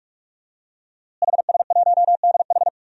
Morse code, heard as a keyed tone, about 42 wpm, 700 Hz. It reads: HS1DH
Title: DXCC-100 Call Signs - 5 Characters - Repeated 2x - 45wpm